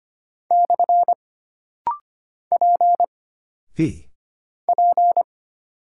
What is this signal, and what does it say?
Morse code, heard as a keyed tone, about 25 wpm, 700 Hz. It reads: / P P
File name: Single Character (Letter:Number:Punctuation:Prosign) - Rapid-Fire - 25wpm